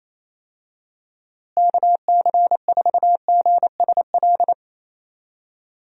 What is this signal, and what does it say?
Morse code, heard as a keyed tone, about 28 wpm, 700 Hz. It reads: KC4GSL
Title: USA Call Signs - 6 Characters - Repeated 2x - 28wpm